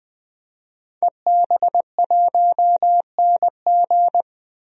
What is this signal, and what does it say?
Morse code, heard as a keyed tone, about 20 wpm, 700 Hz. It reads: EB1NG
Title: DXCC-100 Call Signs - 5 Characters - 20wpm